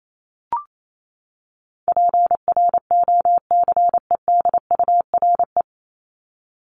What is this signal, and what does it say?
Morse code, heard as a keyed tone, about 28 wpm, 700 Hz. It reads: PROCEDURE
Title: Top 1751-2000 Words 28wpm